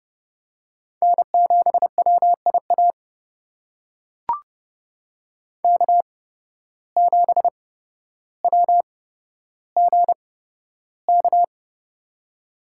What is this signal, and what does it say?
Morse code, heard as a keyed tone, about 30 wpm, spaced as for 13 wpm, 700 Hz. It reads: N7WIA K7WGK
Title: USA Call Signs - 5 Characters - 8x Character Spacing - 30wpm